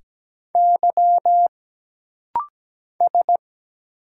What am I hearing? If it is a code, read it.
Morse code, heard as a keyed tone, about 17 wpm, 700 Hz. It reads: Y S
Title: Single Letters - Rapid-Fire 17wpm